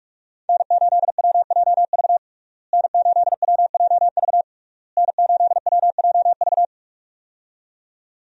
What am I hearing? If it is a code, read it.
Morse code, heard as a keyed tone, about 45 wpm, 700 Hz. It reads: N8WJV N8WJV N8WJV
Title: USA Call Signs - 5 Characters - Repeated 3x - 45wpm